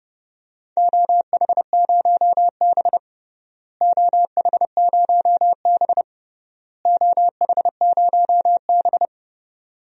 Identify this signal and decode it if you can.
Morse code, heard as a keyed tone, about 29 wpm, 700 Hz. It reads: OH0B OH0B OH0B
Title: DXCC-100 Call Signs - 4 Characters - Repeated 3x - 30wpm